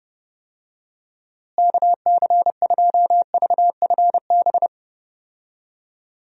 Morse code, heard as a keyed tone, about 30 wpm, 700 Hz. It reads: KC2VFB
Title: USA Call Signs - 6 Characters - Repeated 2x - 30wpm